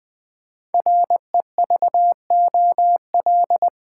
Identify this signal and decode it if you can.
Morse code, heard as a keyed tone, about 20 wpm, 700 Hz. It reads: REVOL
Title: Build A Word - 5 Letters At A Time - 10-Letter Words - 20wpm